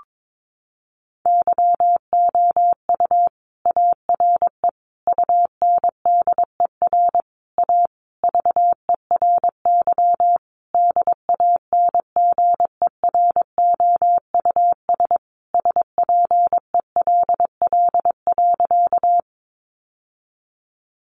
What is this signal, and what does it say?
Morse code, heard as a keyed tone, about 22 wpm, 700 Hz. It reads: YOU ARE UNDER A VERY DANGEROUS SPELL.
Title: Sentences from Top 1200 Words 22wpm